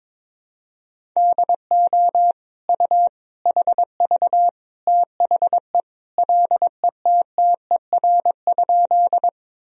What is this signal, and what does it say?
Morse code, heard as a keyed tone, about 21 wpm, 700 Hz. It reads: DO U HV THE LETTER?
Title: Sentences from Top 300 Words and Abbreviations 22wpm